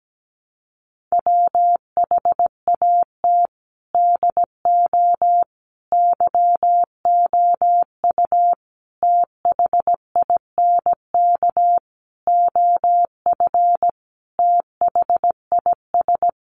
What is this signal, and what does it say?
Morse code, heard as a keyed tone, about 17 wpm, 700 Hz. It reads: WHAT DO YOU THINK OF THIS
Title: Sentences from Top 1200 Words 17wpm